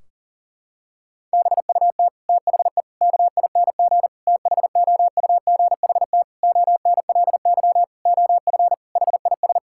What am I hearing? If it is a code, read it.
Morse code, heard as a keyed tone, about 40 wpm, 700 Hz. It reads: BUT THE KING THOUGHT ONLY OF HIS
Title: Sentences from Top 1600 Words 40wpm